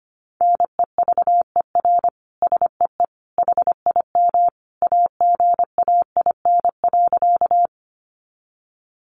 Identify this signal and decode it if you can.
Morse code, heard as a keyed tone, about 25 wpm, 700 Hz. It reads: NEVER SEE HIM AGAIN.
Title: Sentences from Top 300 Words 25wpm